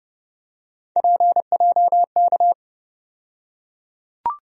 Morse code, heard as a keyed tone, about 30 wpm, 700 Hz. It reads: PJK
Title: USA Call Sign Postfixes - 3 Letters 30wpm